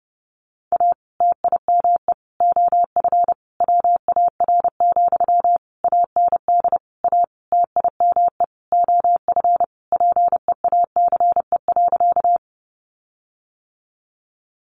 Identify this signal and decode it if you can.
Morse code, heard as a keyed tone, about 30 wpm, 700 Hz. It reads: A TIME OF WAR, AND A TIME OF PEACE.
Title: Sentences from Top 1500 Words 30wpm